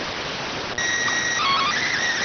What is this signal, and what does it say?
SSTV (slow-scan television), PD 290.